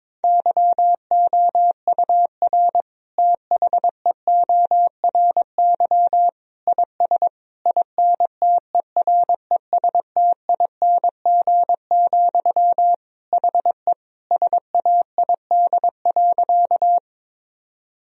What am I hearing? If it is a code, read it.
Morse code, heard as a keyed tone, about 22 wpm, 700 Hz. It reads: YOUR THEORY IS INTERESTING, HE SAID.
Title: Sentences from Top 1000 Words - Review Part 3 - 22wpm